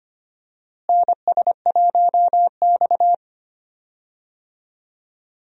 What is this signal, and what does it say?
Morse code, heard as a keyed tone, about 25 wpm, 700 Hz. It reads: NS1X
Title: USA Call Signs - 4 Characters - Repeated 2x - 25wpm